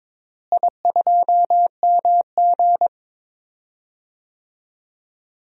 Morse code, heard as a keyed tone, about 22 wpm, 700 Hz. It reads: I2MG
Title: DXCC-100 Call Signs - 4 Characters - Repeated 2x - 22wpm